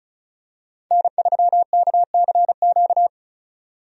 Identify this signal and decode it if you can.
Morse code, heard as a keyed tone, about 36 wpm, 700 Hz. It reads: N3KCQ
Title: USA Call Signs - 5 Characters - 35wpm